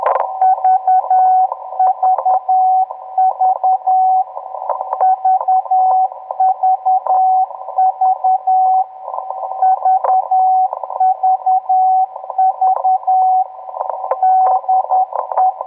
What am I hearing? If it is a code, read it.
Morse code, heard as a keyed tone, about 10 wpm, 750 Hz. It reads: VVV VVV VVV DE